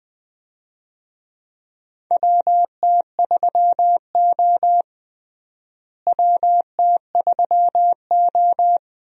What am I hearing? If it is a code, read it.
Morse code, heard as a keyed tone, about 20 wpm, 700 Hz. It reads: WT3O WT3O
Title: USA Call Signs - 4 Characters - Repeated 2x - 20wpm